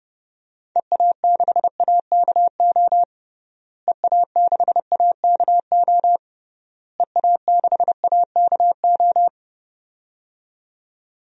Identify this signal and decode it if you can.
Morse code, heard as a keyed tone, about 30 wpm, 700 Hz. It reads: EA6AKO EA6AKO EA6AKO
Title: DXCC-100 Call Signs - 6 Characters - Repeated 3x - 30wpm